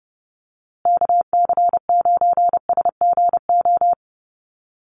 Morse code, heard as a keyed tone, about 30 wpm, 700 Hz. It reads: KC9SGO
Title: USA Call Signs Repeated 3x 30wpm